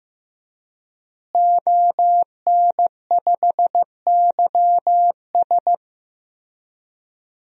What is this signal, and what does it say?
Morse code, heard as a keyed tone, about 15 wpm, 700 Hz. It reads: ON5YS
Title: DXCC-100 Call Signs - 5 Characters - Repeated 3x - 15wpm